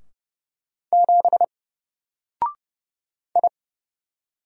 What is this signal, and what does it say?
Morse code, heard as a keyed tone, about 30 wpm, 700 Hz. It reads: 7 I
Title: Single Letter-Number 30wpm